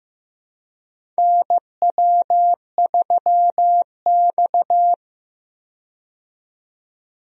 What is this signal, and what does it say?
Morse code, heard as a keyed tone, about 15 wpm, 700 Hz. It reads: NW3X
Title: USA Call Signs - 4 Characters - 15wpm